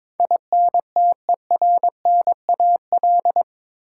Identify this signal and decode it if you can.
Morse code, heard as a keyed tone, about 22 wpm, 700 Hz. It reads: INTERNAL
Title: Sentences from Top 1400 Words - New Word Drill - 22wpm